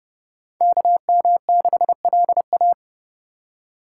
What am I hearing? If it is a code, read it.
Morse code, heard as a keyed tone, about 26 wpm, 700 Hz. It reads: KM6LA